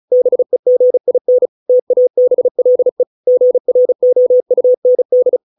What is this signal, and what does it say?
Morse code, heard as a keyed tone, about 34 wpm, 500 Hz. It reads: BEGIN TABLE GROUND